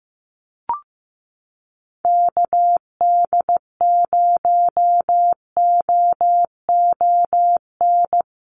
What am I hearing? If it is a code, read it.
Morse code, heard as a keyed tone, about 15 wpm, 700 Hz. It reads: KD0OON
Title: USA Call Signs Repeated 2x 15wpm